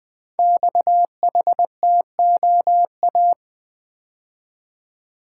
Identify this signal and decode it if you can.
Morse code, heard as a keyed tone, about 20 wpm, 700 Hz. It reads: XHTOA
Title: Code Groups - 5 Letters - 20wpm